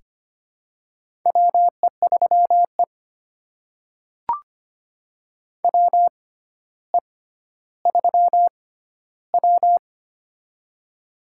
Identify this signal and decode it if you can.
Morse code, heard as a keyed tone, about 25 wpm, spaced as for 13 wpm, 700 Hz. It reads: WE3E WE3W
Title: USA Call Signs - 4 Characters - 6x Character Spacing - 25wpm